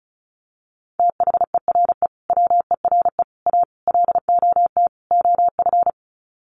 Morse code, heard as a keyed tone, about 34 wpm, 700 Hz. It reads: THERE WERE A LOT OF
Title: Sets of 5 Words 35wpm